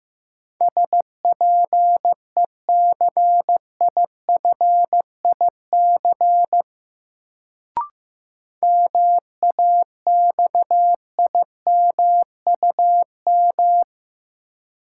Morse code, heard as a keyed tone, about 15 wpm, 700 Hz. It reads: SPECIFIC MAXIMUM
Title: Sentences from Top 1300 Words - New Word Drill Alt - 15wpm